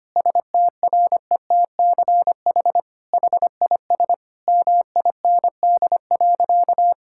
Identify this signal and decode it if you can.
Morse code, heard as a keyed tone, about 25 wpm, 700 Hz. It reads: STRETCH HIS MIND.
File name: practice / 25wpm / All-Age Appropriate Jokes - 25wpm